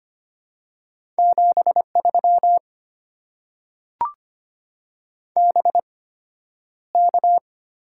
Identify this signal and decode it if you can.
Morse code, heard as a keyed tone, about 25 wpm, spaced as for 11 wpm, 690 Hz. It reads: 73 BK
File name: Top 90 QSO Elements - 8x Character Spacing - 25wpm